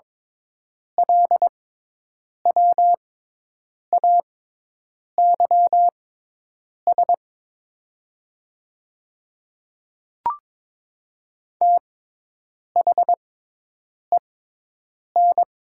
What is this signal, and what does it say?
Morse code, heard as a keyed tone, about 22 wpm, spaced as for 8 wpm, 700 Hz. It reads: LWAYS THEN
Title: 6x Character Spacing - Top 300 Words - 22wpm